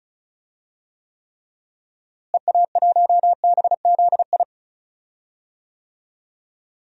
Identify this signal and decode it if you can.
Morse code, heard as a keyed tone, about 34 wpm, 700 Hz. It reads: EA1BZI
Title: DXCC-100 Call Signs - 6 Characters - Repeated 3x - 35wpm